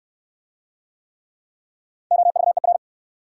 Morse code, heard as a keyed tone, about 70 wpm, 690 Hz. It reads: OVER